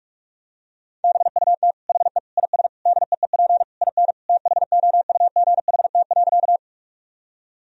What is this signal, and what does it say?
Morse code, heard as a keyed tone, about 43 wpm, 700 Hz. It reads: BUT HE IS DEEP IN THOUGHT.
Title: Sentences from Top 1000 Words - Review Part 3 - 45wpm